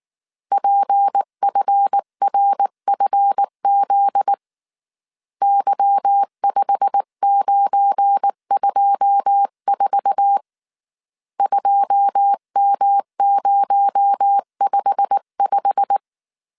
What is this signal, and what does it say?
Morse code, heard as a keyed tone, about 19 wpm, 800 Hz. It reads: PFRFZ Y5924 2M055
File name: CW / DET / 1 9 1500C S C 18wpm 2539